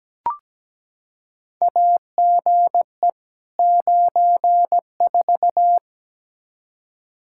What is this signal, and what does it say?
Morse code, heard as a keyed tone, about 17 wpm, 700 Hz. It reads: AGE 94
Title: Practice QSO - Exchange 3 - More Info - 17wpm